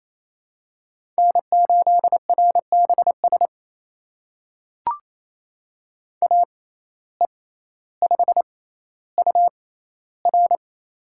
Morse code, heard as a keyed tone, about 28 wpm, spaced as for 14 wpm, 700 Hz. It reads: N8RBS AE5UR
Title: USA Call Signs - 5 Characters - 6x Character Spacing - 28wpm